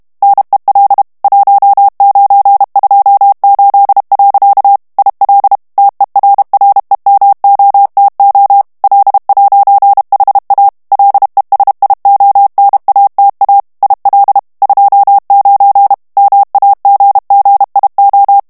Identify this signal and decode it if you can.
Morse code, heard as a keyed tone, about 32 wpm, 800 Hz. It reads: NEL 1928. IL TERREMOTO L'HA LESIONATA IL 29 MAGGIO
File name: morse mp3 / ITALIEN texte clair 1 vitesse 30 mots minute